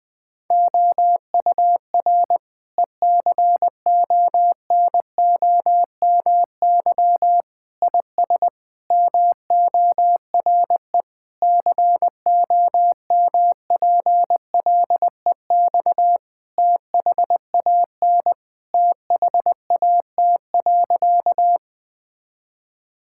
Morse code, heard as a keyed tone, about 20 wpm, 700 Hz. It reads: OUR ECONOMY IS MORE COMPLEX THAN THAT.
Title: Sentences from Top 1000 Words - Review Part 6 - 20wpm